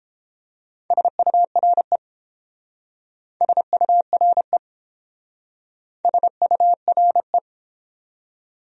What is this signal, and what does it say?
Morse code, heard as a keyed tone, about 29 wpm, 700 Hz. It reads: SURE SURE SURE